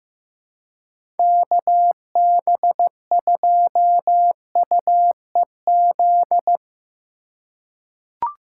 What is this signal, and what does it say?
Morse code, heard as a keyed tone, about 15 wpm, 700 Hz. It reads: KB2UEZ
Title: USA Call Signs - 6 Characters - 2x Character Spacing - 15wpm